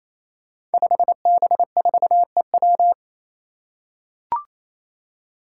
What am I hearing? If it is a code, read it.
Morse code, heard as a keyed tone, about 28 wpm, 700 Hz. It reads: 5B4EW
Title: Five Challenging Letter-Number Combinations 28wpm